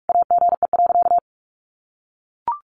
Morse code, heard as a keyed tone, about 44 wpm, 700 Hz. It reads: AGE.
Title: Sentences from Top 1300 Words 45wpm